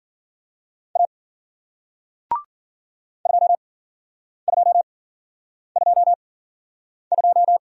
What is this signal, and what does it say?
Morse code, heard as a keyed tone, about 54 wpm, spaced as for 9 wpm, 700 Hz. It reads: A 2222